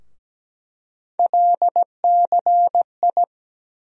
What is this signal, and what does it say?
Morse code, heard as a keyed tone, about 17 wpm, 700 Hz. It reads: LCI
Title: USA Call Sign Postfixes - 3 Letters 17wpm